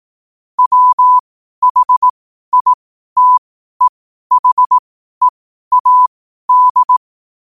Morse code, heard as a keyed tone, about 18 wpm, spaced as for 12 wpm, 1000 Hz. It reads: WHITEHEAD
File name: qotdfiles / 12wpm